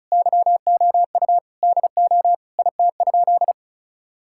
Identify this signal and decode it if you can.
Morse code, heard as a keyed tone, about 36 wpm, 700 Hz. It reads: YOU DO IT?
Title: Sentences from Top 1000 Words - Review Part 5 - 35wpm